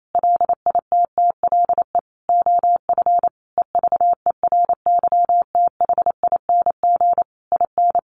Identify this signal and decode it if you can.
Morse code, heard as a keyed tone, about 28 wpm, 700 Hz. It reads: LITTLE OF EVERYTHING IN
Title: Sentences from Top 700 Words 28wpm